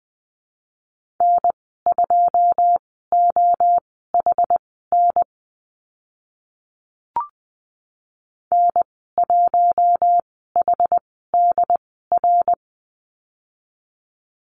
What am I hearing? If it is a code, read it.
Morse code, heard as a keyed tone, about 20 wpm, spaced as for 14 wpm, 700 Hz. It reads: N2OHN N1HDR
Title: 082 Lesson 14g - Number 2 - Call Signs 20wpm